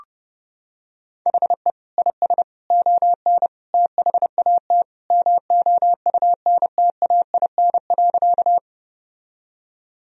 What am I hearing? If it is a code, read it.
Morse code, heard as a keyed tone, about 30 wpm, 700 Hz. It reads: HE IS ON THAT MOUNTAIN.
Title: Sentences from Top 1000 Words - Review Part 1 - 30wpm